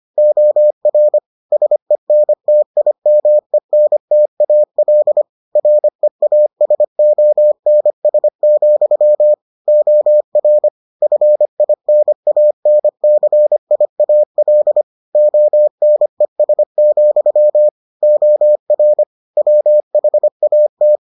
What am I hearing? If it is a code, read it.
Morse code, heard as a keyed tone, about 25 wpm, 600 Hz. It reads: OR SENTIMENTAL REASONS, OR FINANCIAL ONES, OR WHAT